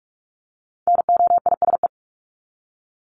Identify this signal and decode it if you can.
Morse code, heard as a keyed tone, about 44 wpm, 700 Hz. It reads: NOISE
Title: Top 1501-1750 Words 45wpm